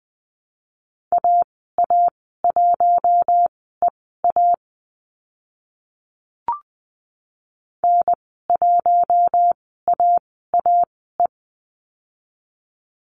Morse code, heard as a keyed tone, about 20 wpm, spaced as for 14 wpm, 700 Hz. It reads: AA1EA N1AAE
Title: 042 Lesson 8g - Number 1 - Call Signs 20wpm